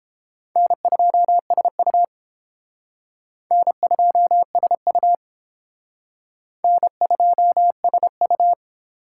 Morse code, heard as a keyed tone, about 30 wpm, 700 Hz. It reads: N2SU N2SU N2SU